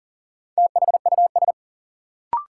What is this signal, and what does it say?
Morse code, heard as a keyed tone, about 40 wpm, 700 Hz. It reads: THUS
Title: Sentences from Top 800 Words - New Word Drill - 40wpm